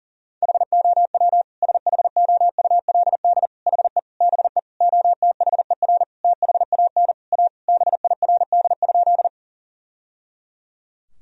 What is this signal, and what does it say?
Morse code, heard as a keyed tone, about 40 wpm, 700 Hz. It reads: HOW SHOULD HE BE OTHER THAN A BIRD?
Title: Sentences from Top 1000 Words - Review Part 2 - 40wpm